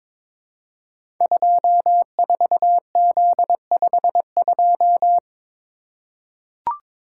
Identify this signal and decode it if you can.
Morse code, heard as a keyed tone, about 23 wpm, 700 Hz. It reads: 24Z52
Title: Code Groups - 5 Letter:Numbers - 22wpm